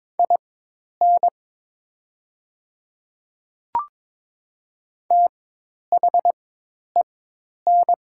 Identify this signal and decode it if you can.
Morse code, heard as a keyed tone, about 22 wpm, spaced as for 10 wpm, 700 Hz. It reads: IN THEN
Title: 4x Character Spacing - Top 100 Words - 22wpm